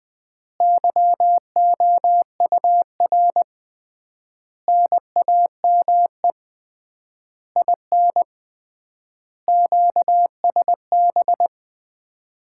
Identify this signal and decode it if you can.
Morse code, heard as a keyed tone, about 21 wpm, 700 Hz. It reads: YOUR NAME IN QSB
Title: Common QSO Phrases - 3x Word-Spacing - 20wpm